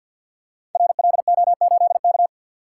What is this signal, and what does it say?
Morse code, heard as a keyed tone, about 46 wpm, 700 Hz. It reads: RLO8K
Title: Code Groups - 5 Letter:Numbers - 50wpm